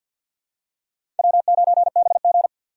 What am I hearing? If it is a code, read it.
Morse code, heard as a keyed tone, about 53 wpm, 700 Hz. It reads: W9BG
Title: USA Call Signs - 4 Characters - 6x Character Spacing - 50wpm